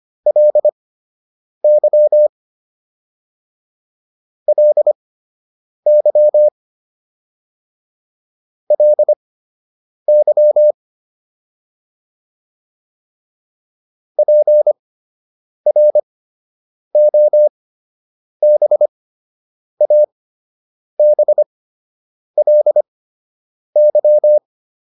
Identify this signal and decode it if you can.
Morse code, heard as a keyed tone, about 25 wpm, spaced as for 8 wpm, 600 Hz. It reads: LY LY LY PROBABLY